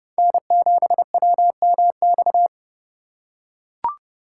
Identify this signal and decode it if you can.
Morse code, heard as a keyed tone, about 27 wpm, 700 Hz. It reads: N7WMX